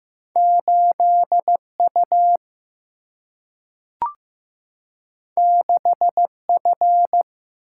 Morse code, heard as a keyed tone, about 15 wpm, 700 Hz. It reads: 8U 6F
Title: Single-Letter Single-Number 15wpm